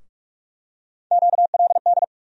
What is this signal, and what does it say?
Morse code, heard as a keyed tone, about 46 wpm, 700 Hz. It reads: QLD
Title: USA Call Sign Postfixes - 3 Letters 45wpm